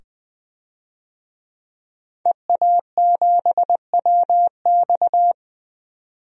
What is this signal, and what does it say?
Morse code, heard as a keyed tone, about 20 wpm, 700 Hz. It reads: EA7WX
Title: DXCC-100 Call Signs - 5 Characters - 20wpm